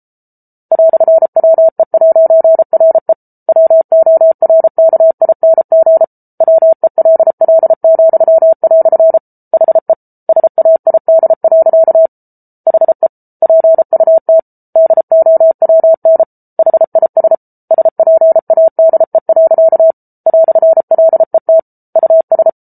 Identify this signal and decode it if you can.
Morse code, heard as a keyed tone, about 34 wpm, 650 Hz. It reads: "WE'RE WORKING WELL," HE SAID. HE PUT DOWN HIS SPADE. "LET US